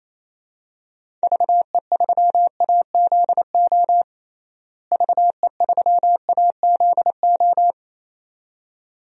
Morse code, heard as a keyed tone, about 28 wpm, 700 Hz. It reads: VE3AZO VE3AZO
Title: DXCC-100 Call Signs - 6 Characters - Repeated 2x - 28wpm